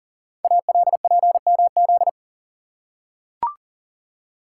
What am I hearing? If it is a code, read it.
Morse code, heard as a keyed tone, about 40 wpm, 700 Hz. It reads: ALPMZ
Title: Code Groups - 5 Letter:Numbers - 40wpm